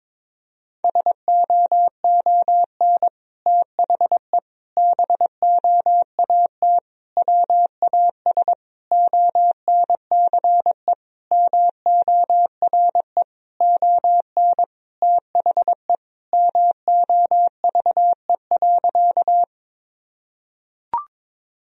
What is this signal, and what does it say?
Morse code, heard as a keyed tone, about 22 wpm, 700 Hz. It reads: SOON THE BOAT WAS ONCE MORE ON THE MOVE.
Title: Sentences from Top 1000 Words - Review Part 3 - 22wpm